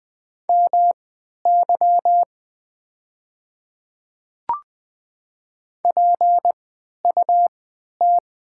Morse code, heard as a keyed tone, about 20 wpm, spaced as for 11 wpm, 700 Hz. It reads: MY PUT
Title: 3x Character Spacing - Top 300 Words - 20wpm